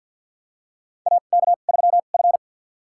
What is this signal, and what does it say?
Morse code, heard as a keyed tone, about 47 wpm, spaced as for 35 wpm, 700 Hz. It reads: AK3F
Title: USA Call Signs - 4 Characters - 2x Character Spacing - 50wpm